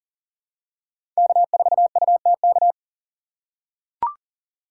Morse code, heard as a keyed tone, about 40 wpm, 700 Hz. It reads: K4UTK
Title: USA Call Signs - 5 Characters - 3x Character Spacing - 40wpm